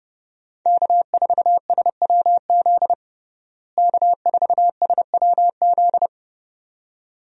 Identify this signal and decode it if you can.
Morse code, heard as a keyed tone, about 32 wpm, 700 Hz. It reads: K4SWZ K4SWZ
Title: USA Call Signs - 5 Characters - Repeated 2x - 30wpm